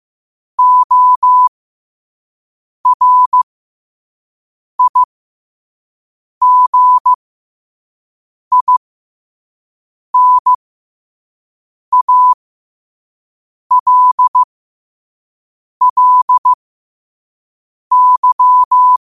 Morse code, heard as a keyed tone, about 15 wpm, spaced as for 5 wpm, 1000 Hz. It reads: ORIGINALLY
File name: qotdfiles / 5wpm